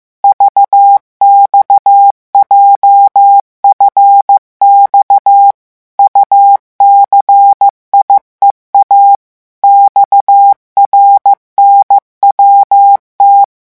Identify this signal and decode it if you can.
Morse code, heard as a keyed tone, about 15 wpm, 800 Hz. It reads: VXJFX UCIEA XRNWT
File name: morse mp3 / revision7-vitesse 14 mots